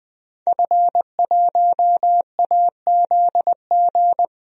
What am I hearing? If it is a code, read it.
Morse code, heard as a keyed tone, about 20 wpm, 700 Hz. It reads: F1AZG
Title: DXCC-100 Call Signs - 5 Characters - 20wpm